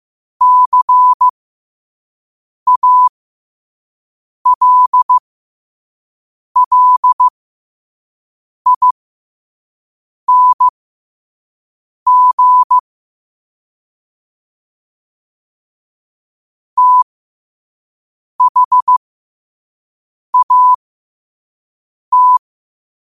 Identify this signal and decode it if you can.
Morse code, heard as a keyed tone, about 15 wpm, spaced as for 5 wpm, 1000 Hz. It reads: CALLING THAT
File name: qotdfiles / 5wpm